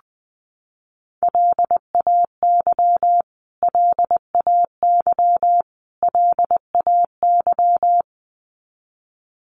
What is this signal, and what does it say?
Morse code, heard as a keyed tone, about 20 wpm, 700 Hz. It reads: LAY LAY LAY
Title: Top 500 Words Repeated 20wpm